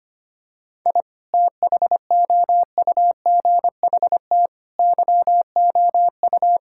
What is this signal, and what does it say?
Morse code, heard as a keyed tone, about 25 wpm, 700 Hz. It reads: I THOUGHT YOU
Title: Sentences from Top 1000 Words - Review Part 5 - 25wpm